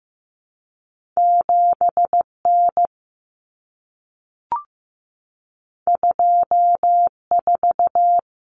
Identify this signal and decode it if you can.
Morse code, heard as a keyed tone, about 15 wpm, 700 Hz. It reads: 7N 24
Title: Common Cut Numbers - 2-Digit 15wpm